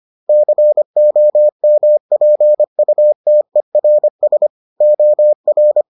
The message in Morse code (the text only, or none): COMPUTERS OR